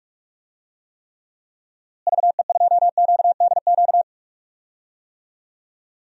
Morse code, heard as a keyed tone, about 46 wpm, 700 Hz. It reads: VE2QDQ
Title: DXCC-100 Call Signs - 6 Characters - Repeated 2x - 45wpm